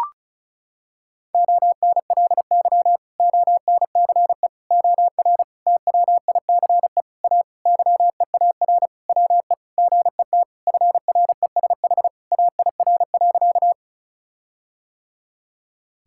Morse code, heard as a keyed tone, about 36 wpm, 700 Hz. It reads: ONLY ONCE OR TWICE A YEAR WE GET FRESH AIR.
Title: Sentences from Top 1600 Words 35wpm